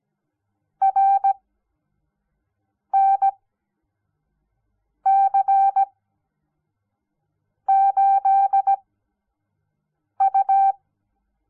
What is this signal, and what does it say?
Morse code, heard as a keyed tone, about 17 wpm, spaced as for 5 wpm, 780 Hz. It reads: RNC8U
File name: Onlinelehrgang / morsen / DJ4UF Morsen06